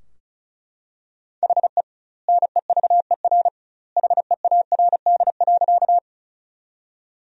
Morse code, heard as a keyed tone, about 36 wpm, 700 Hz. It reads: HE NEVER HEARD.
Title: 2x Word-Spacing - Sentences from Top 400 Words - 35wpm